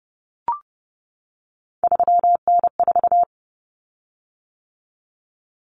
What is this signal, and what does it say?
Morse code, heard as a keyed tone, about 30 wpm, 700 Hz. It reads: 3N4
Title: Common Cut Numbers - 3-Digit 30wpm